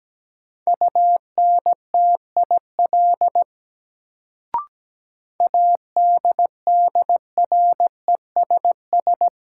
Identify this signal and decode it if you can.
Morse code, heard as a keyed tone, about 17 wpm, 700 Hz. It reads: UNTIL ADDRESS
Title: Sets of 5 Words Encore - New Word Drill - 17wpm